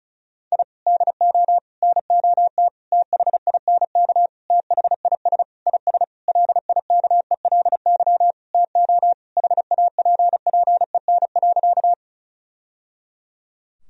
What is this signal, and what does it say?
Morse code, heard as a keyed tone, about 34 wpm, 700 Hz. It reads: I DO NOT THINK THIS IS LIKELY TO HAPPEN.
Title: Sentences from Top 1000 Words - Review Part 4 - 35wpm